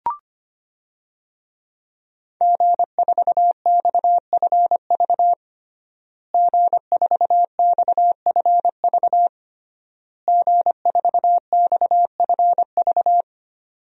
Morse code, heard as a keyed tone, about 25 wpm, 700 Hz. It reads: G4XFV G4XFV G4XFV
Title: DXCC-100 Call Signs - 5 Characters - Repeated 3x - 25wpm